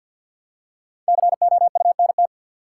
Morse code, heard as a keyed tone, about 51 wpm, 700 Hz. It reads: COUNT
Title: Five Letter Words - 50wpm